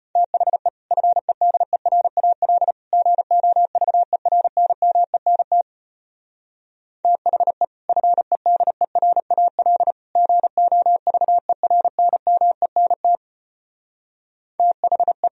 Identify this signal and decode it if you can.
Morse code, heard as a keyed tone, about 32 wpm, 700 Hz. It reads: THE FEDERAL GOVERNMENT THE FEDERAL GOVERNMENT THE